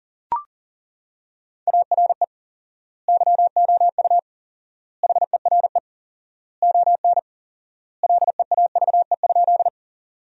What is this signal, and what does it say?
Morse code, heard as a keyed tone, about 40 wpm, 700 Hz. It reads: ARE YOU HERE ON LEAVE?
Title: 4x Word-Spacing - Sentences from Top 400 Words - 40wpm